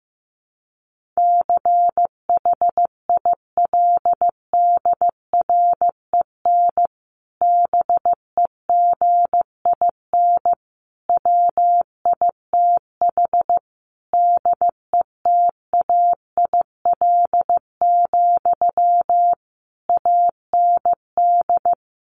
Morse code, heard as a keyed tone, about 15 wpm, 700 Hz. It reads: CHILDREN BEGIN WITH DETAIL, AND
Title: Sentences from Top 1500 Words 15wpm